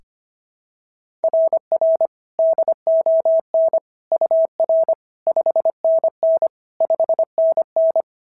Morse code, heard as a keyed tone, about 25 wpm, 690 Hz. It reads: RR DON UR 5NN 5NN
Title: SOTA QSO Practice 25wpm